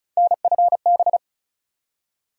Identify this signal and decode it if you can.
Morse code, heard as a keyed tone, about 36 wpm, 700 Hz. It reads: NFB
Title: USA Call Sign Postfixes - 3 Letters 35wpm